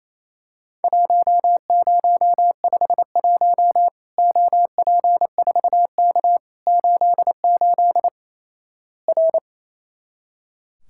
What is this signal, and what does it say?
Morse code, heard as a keyed tone, about 28 wpm, 700 Hz. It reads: 1051 OP4K 88 R
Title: Worked All Europe DX Contest - QTC Exchange - 28wpm